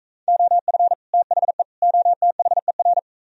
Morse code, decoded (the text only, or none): OF THE OTHER